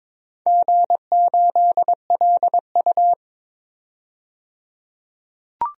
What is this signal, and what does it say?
Morse code, heard as a keyed tone, about 21 wpm, 700 Hz. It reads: G8LU
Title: DXCC-100 Call Signs - 4 Characters - 22wpm